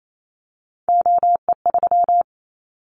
Morse code, heard as a keyed tone, about 28 wpm, 700 Hz. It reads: OE3
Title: Callsign Prefixes for Top 25 DXCC - 28wpm